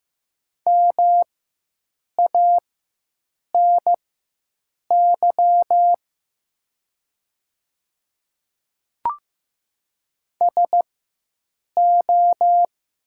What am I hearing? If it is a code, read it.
Morse code, heard as a keyed tone, about 15 wpm, spaced as for 7 wpm, 700 Hz. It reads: MANY SO
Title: 4x Character Spacing - Top 100 Words - 15wpm